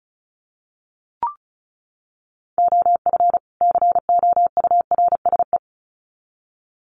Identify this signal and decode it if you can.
Morse code, heard as a keyed tone, about 34 wpm, 700 Hz. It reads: OF COURSE
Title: Sets of 2 Words 35wpm